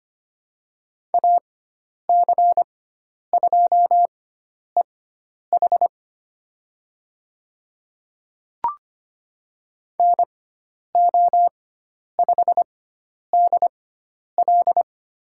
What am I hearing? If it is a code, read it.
Morse code, heard as a keyed tone, about 25 wpm, spaced as for 10 wpm, 700 Hz. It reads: AC2EH NO5DL
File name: Lesson 16g - Letter c - Call Signs 25wpm